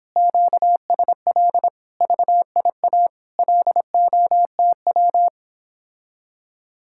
Morse code, heard as a keyed tone, about 31 wpm, 700 Hz. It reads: QSL VIA LOTW